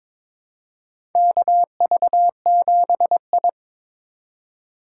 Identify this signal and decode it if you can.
Morse code, heard as a keyed tone, about 22 wpm, 700 Hz. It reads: KV7I
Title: USA Call Signs - 4 Characters - 4x Character Spacing - 22wpm